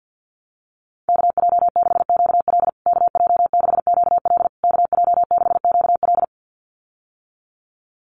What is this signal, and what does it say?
Morse code, heard as a keyed tone, about 50 wpm, 700 Hz. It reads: KJ6QL KJ6QL KJ6QL
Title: USA Call Signs Repeated 3x 50wpm